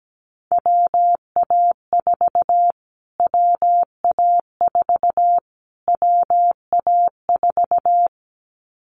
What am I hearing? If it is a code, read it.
Morse code, heard as a keyed tone, about 17 wpm, 700 Hz. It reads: WA4 WA4 WA4
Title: Callsign Prefixes for Top 25 DXCC - Repeated 3x - 17wpm